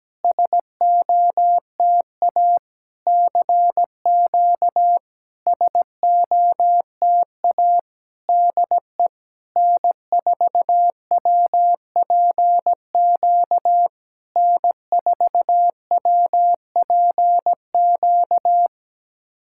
Morse code, heard as a keyed tone, about 17 wpm, 700 Hz. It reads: SOTA CQ SOTA DE N4WPQ N4WPQ
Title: SOTA QSO Practice 17wpm